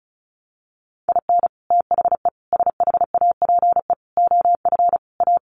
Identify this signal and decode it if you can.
Morse code, heard as a keyed tone, about 34 wpm, 700 Hz. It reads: IN THE SHAPE OF A
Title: Sets of 5 Words Encore 35wpm